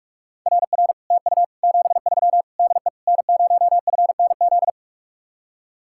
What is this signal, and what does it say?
Morse code, heard as a keyed tone, about 44 wpm, 700 Hz. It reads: RR TU 73 DE N0FNZ